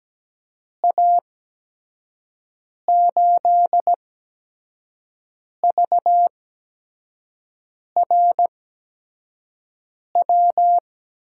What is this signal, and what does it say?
Morse code, heard as a keyed tone, about 17 wpm, spaced as for 5 wpm, 700 Hz. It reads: A8VRW
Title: USA Call Signs - 6 Characters - 8x Character Spacing - 17wpm